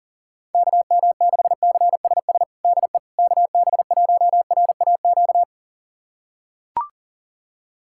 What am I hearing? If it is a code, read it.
Morse code, heard as a keyed tone, about 40 wpm, 700 Hz. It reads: KM6CSS DE KB1RAQ
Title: USA Callsign DE Callsign 40wpm